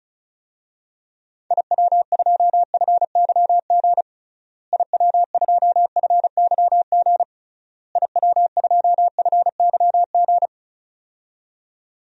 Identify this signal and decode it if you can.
Morse code, heard as a keyed tone, about 36 wpm, 700 Hz. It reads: IW2FYG IW2FYG IW2FYG
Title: DXCC-100 Call Signs - 6 Characters - Repeated 3x - 35wpm